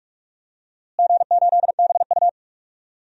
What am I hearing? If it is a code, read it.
Morse code, heard as a keyed tone, about 42 wpm, 700 Hz. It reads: G8BU
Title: DXCC-100 Call Signs - 4 Characters - Repeated 3x - 45wpm